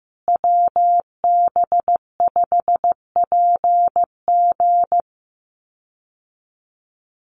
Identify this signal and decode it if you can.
Morse code, heard as a keyed tone, about 15 wpm, 700 Hz. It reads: WB5PG
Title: USA Call Signs Repeated 2x 15wpm